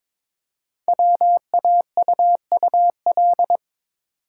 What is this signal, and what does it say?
Morse code, heard as a keyed tone, about 23 wpm, 700 Hz. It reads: WAUUL
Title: Code Groups - 5 Letters - 22wpm